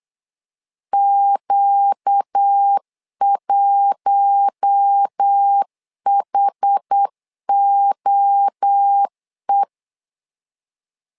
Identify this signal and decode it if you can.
Morse code, heard as a keyed tone, about 8 wpm, 800 Hz. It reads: Q1HOE
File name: LEZIONI / 4 Q1 8wpm 3035